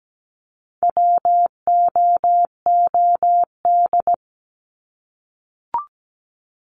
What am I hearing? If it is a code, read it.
Morse code, heard as a keyed tone, about 17 wpm, 700 Hz. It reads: WOOD
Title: Top 300 Words 17wpm